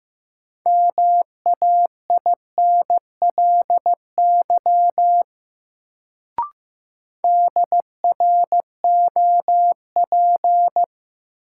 Morse code, heard as a keyed tone, about 15 wpm, 700 Hz. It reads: MAINLY DROP
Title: Sentences from Top 1400 Words - New Word Drill Alt - 15wpm